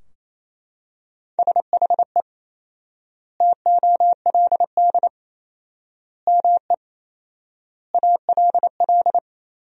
Morse code, heard as a keyed tone, about 28 wpm, 700 Hz. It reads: SHE TOLD ME ALL
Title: 4x Word-Spacing - Sentences from Top 400 Words - 28wpm